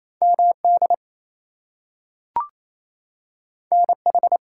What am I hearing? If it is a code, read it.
Morse code, heard as a keyed tone, about 28 wpm, 700 Hz. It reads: MD NH
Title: US States and Canadian Provinces 28wpm